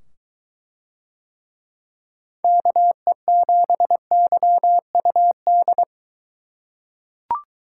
Morse code, heard as a keyed tone, about 23 wpm, 700 Hz. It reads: KE7YUD